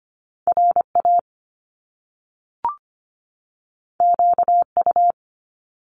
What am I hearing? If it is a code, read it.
Morse code, heard as a keyed tone, about 25 wpm, 700 Hz. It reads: RA QU
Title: Top Two Letter Combinations in Words 25wpm